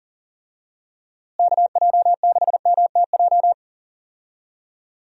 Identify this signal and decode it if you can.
Morse code, heard as a keyed tone, about 40 wpm, 700 Hz. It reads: KJ6MTJ
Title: USA Call Signs - 6 Characters - Repeated 3x - 40wpm